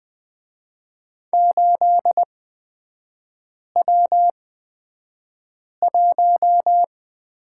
Morse code, heard as a keyed tone, about 20 wpm, spaced as for 5 wpm, 700 Hz. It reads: 8W1
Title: 250 Lesson 40e - BK - Three Characters 20wpm